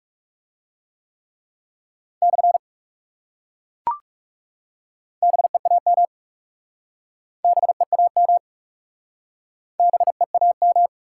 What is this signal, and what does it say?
Morse code, heard as a keyed tone, about 40 wpm, 700 Hz. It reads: / BEAM BEAM BEAM